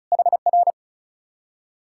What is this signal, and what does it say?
Morse code, heard as a keyed tone, about 46 wpm, 700 Hz. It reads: HR